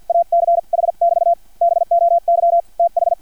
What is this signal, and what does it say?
Morse code, heard as a keyed tone, about 50 wpm, spaced as for 43 wpm, 680 Hz. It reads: ACH <BT> BOY TH
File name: archived / CRSnet